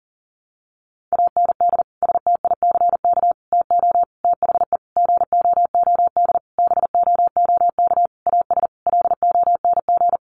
Morse code, heard as a keyed tone, about 40 wpm, 700 Hz. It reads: AND STICK TO THE GOOD BOOK AS LONG
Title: Sentences from Top 1200 Words 40wpm